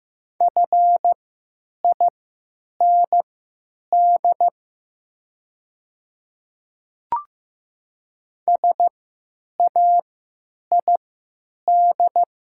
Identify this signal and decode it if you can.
Morse code, heard as a keyed tone, about 15 wpm, spaced as for 9 wpm, 700 Hz. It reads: FIND SAID
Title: 3x Character Spacing - Top 100 Words - 15wpm